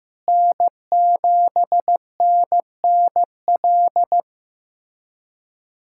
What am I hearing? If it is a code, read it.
Morse code, heard as a keyed tone, about 16 wpm, 700 Hz. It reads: N7NNL
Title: USA Callsign DE Callsign - 5 Characters - 15wpm